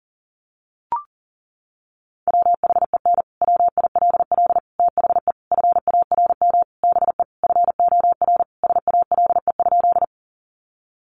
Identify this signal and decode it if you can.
Morse code, heard as a keyed tone, about 40 wpm, 700 Hz. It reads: WHEN WILL THE FARM BE FOR SALE?
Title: Sentences from Top 1400 Words 40wpm